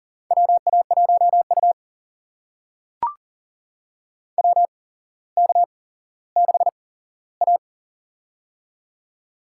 Morse code, heard as a keyed tone, about 40 wpm, spaced as for 17 wpm, 700 Hz. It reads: WA1U WK6A
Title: USA Call Signs - 4 Characters - 8x Character Spacing - 40wpm